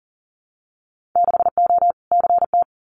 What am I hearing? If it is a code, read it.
Morse code, heard as a keyed tone, about 40 wpm, 700 Hz. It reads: 6O CT
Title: Winter Field Day Exchanges Encore 40wpm